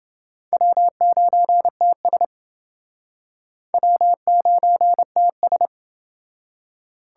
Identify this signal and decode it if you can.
Morse code, heard as a keyed tone, about 30 wpm, 700 Hz. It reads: W9TS W9TS